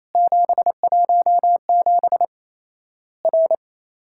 Morse code, heard as a keyed tone, about 27 wpm, 700 Hz. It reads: 717 R
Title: Worked All Europe DX Contest - QTC Exchange - 28wpm